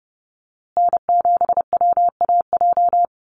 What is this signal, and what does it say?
Morse code, heard as a keyed tone, about 30 wpm, 700 Hz. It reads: N7WAJ
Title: USA Call Signs Repeated 2x 30wpm